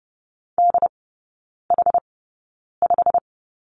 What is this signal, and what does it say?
Morse code, heard as a keyed tone, about 30 wpm, spaced as for 9 wpm, 700 Hz. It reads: DH5
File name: course / 30:10wpm / 087 Lesson 15e - Number 5 - Three Characters 30wpm